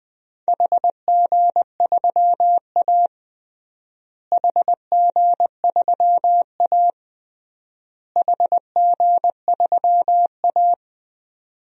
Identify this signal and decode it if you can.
Morse code, heard as a keyed tone, about 20 wpm, 700 Hz. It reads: HG3A HG3A HG3A
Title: DXCC-100 Call Signs - 4 Characters - Repeated 3x - 20wpm